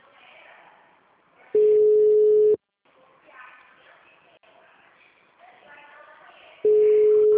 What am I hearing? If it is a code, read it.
Ringback tone (Europe, 425 Hz).